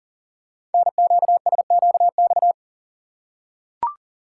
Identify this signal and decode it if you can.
Morse code, heard as a keyed tone, about 43 wpm, 700 Hz. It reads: NQSQX
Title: Code Groups - 5 Letters - 40wpm